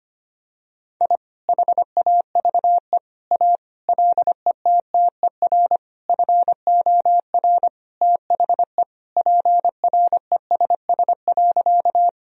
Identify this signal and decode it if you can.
Morse code, heard as a keyed tone, about 25 wpm, 700 Hz. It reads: I HAVE A LETTER FOR THE PRESS.
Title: Sentences from Top 1000 Words - Review Part 1 - 25wpm